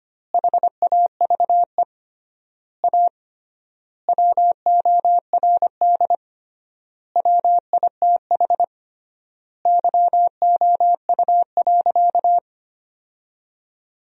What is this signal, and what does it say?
Morse code, heard as a keyed tone, about 25 wpm, 700 Hz. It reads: HAVE A WORD WITH YOU.
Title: 3x Word-Spacing - Sentences from Top 300 Words - 25wpm